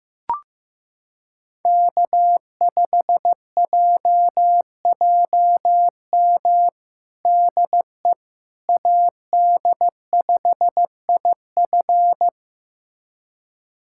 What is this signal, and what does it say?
Morse code, heard as a keyed tone, about 16 wpm, 700 Hz. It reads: K5JJM DE AD5IF
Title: USA Callsign DE Callsign - 5 Characters - 15wpm